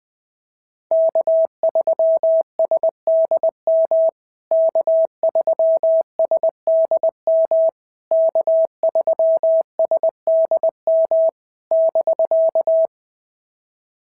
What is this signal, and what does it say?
Morse code, heard as a keyed tone, about 21 wpm, 650 Hz. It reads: K3SDM K3SDM K3SDM <BK>
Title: POTA QSO Practice - 20wpm